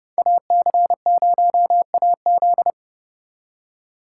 Morse code, heard as a keyed tone, about 35 wpm, 700 Hz. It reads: AC0AZ
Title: Speed Racing - USA 2x2 Call Signs 30wpm